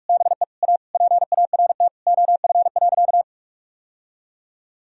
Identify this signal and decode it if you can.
Morse code, heard as a keyed tone, about 41 wpm, 700 Hz. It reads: BE A PART OF.
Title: Sentences from Top 1000 Words - Review Part 1 - 45wpm